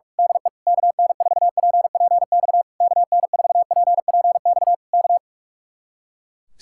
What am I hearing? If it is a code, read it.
Morse code, heard as a keyed tone, about 45 wpm, 700 Hz. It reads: DE KN4PPX KN4PPX K
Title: Speed-Racing Common QSO Phrases Encore - 45wpm